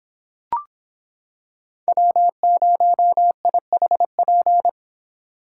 Morse code, heard as a keyed tone, about 26 wpm, 700 Hz. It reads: W0IHP